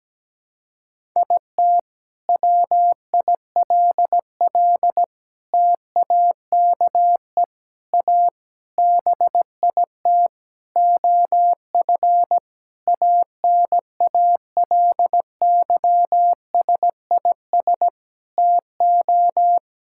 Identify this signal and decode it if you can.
Morse code, heard as a keyed tone, about 17 wpm, 700 Hz. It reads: IT WILL TAKE A BIT OF ANALYSIS TO
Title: Sentences from Top 1000 Words - Review Part 5 - 17wpm